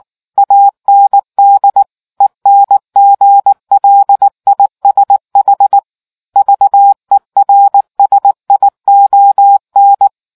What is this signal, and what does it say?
Morse code, heard as a keyed tone, about 19 wpm, 800 Hz. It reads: AND ENGLISH VERSION